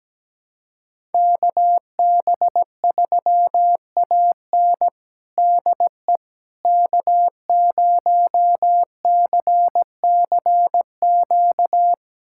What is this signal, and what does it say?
Morse code, heard as a keyed tone, about 17 wpm, 700 Hz. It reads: KB3AN DE K0CCQ
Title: USA Callsign DE Callsign - 5 Characters - 17wpm